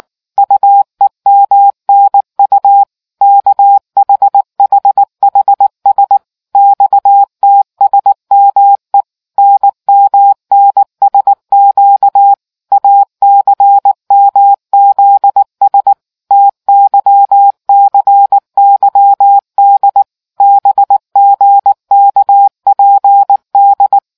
Morse code, heard as a keyed tone, about 18 wpm, 800 Hz. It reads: UEMNU KHHHS XTSME NMNSQ ACMZS TYCYD BGKPD